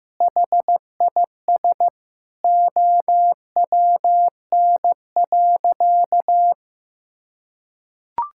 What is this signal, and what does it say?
Morse code, heard as a keyed tone, about 15 wpm, 700 Hz. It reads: HIS OWN.
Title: Sentences from Top 1000 Words - Review Part 6 - 15wpm